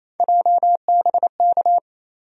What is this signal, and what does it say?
Morse code, heard as a keyed tone, about 28 wpm, 700 Hz. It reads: JBK
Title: USA Call Sign Postfixes - 3 Letters 28wpm